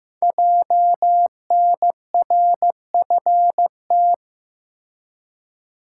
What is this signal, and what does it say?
Morse code, heard as a keyed tone, about 15 wpm, 700 Hz. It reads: JNRFT
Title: Code Groups - 5 Letter:Numbers - 15wpm